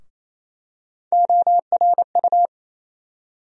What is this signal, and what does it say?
Morse code, heard as a keyed tone, about 28 wpm, 700 Hz. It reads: ORU
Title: USA Call Sign Postfixes - 3 Letters 28wpm